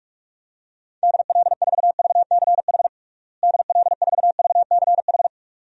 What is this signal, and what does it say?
Morse code, heard as a keyed tone, about 46 wpm, 700 Hz. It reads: DL4VCH DL4VCH
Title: DXCC-100 Call Signs - 6 Characters - Repeated 2x - 45wpm